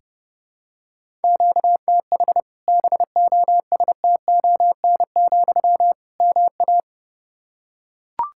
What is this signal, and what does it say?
Morse code, heard as a keyed tone, about 32 wpm, 700 Hz. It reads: QTH BOSTON, MA
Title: Practice QSO - Exchange 1 - Beginning - 30wpm